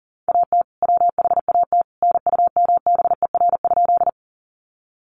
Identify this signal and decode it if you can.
Morse code, heard as a keyed tone, about 40 wpm, 700 Hz. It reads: AT WHAT NUMBER?
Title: Sentences from Top 100 Words 40wpm